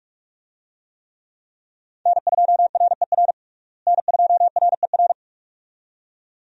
Morse code, heard as a keyed tone, about 46 wpm, 700 Hz. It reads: N2RER N2RER
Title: USA Call Signs - 5 Characters - Repeated 2x - 45wpm